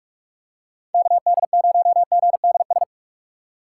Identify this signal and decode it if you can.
Morse code, heard as a keyed tone, about 46 wpm, 700 Hz. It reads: KD0GDS
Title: USA Call Signs - 6 Characters - 2x Character Spacing - 45wpm